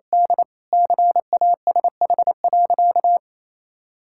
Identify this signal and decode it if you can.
Morse code, heard as a keyed tone, about 28 wpm, 700 Hz. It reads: D CASH.
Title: Sentences from Top 1000 Words - Review Part 5 - 28wpm